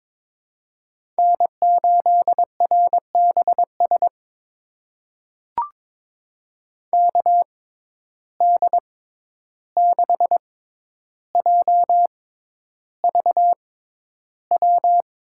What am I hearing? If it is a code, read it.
Morse code, heard as a keyed tone, about 22 wpm, spaced as for 11 wpm, 700 Hz. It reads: N8RBS KD6JVW
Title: USA Call Signs - 6x Character Spacing - 22wpm